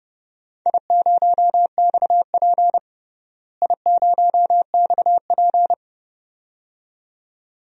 Morse code, heard as a keyed tone, about 30 wpm, 700 Hz. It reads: I0XP I0XP
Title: DXCC-100 Call Signs - 4 Characters - Repeated 2x - 30wpm